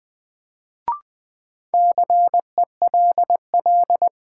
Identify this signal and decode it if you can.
Morse code, heard as a keyed tone, about 20 wpm, 700 Hz. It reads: CELL
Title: Sentences from Top 1100 Words - New Word Drill - 20wpm